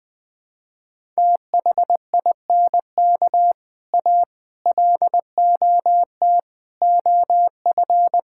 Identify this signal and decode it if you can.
Morse code, heard as a keyed tone, about 20 wpm, 700 Hz. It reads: THINK A LOT OF
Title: Build A Phrase - 5-Word Phrases - 20wpm